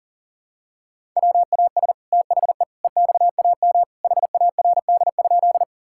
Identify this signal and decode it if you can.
Morse code, heard as a keyed tone, about 40 wpm, 700 Hz. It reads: WAS THE EXAM HARD?
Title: Sentences from Top 1000 Words - Review Part 5 - 40wpm